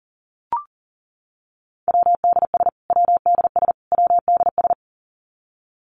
Morse code, heard as a keyed tone, about 40 wpm, 700 Hz. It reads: WDS WDS WDS
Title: Abbreviations Repeated 3x - 40wpm